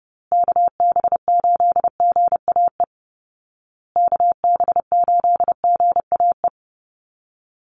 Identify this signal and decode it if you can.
Morse code, heard as a keyed tone, about 30 wpm, 700 Hz. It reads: KB8GAE KB8GAE
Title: Call Signs Repeated 2x - 30wpm